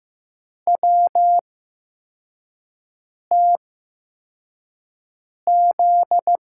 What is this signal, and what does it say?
Morse code, heard as a keyed tone, about 15 wpm, spaced as for 4 wpm, 700 Hz. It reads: WTZ
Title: USA Call Signs - 8x Character Spacing - 15wpm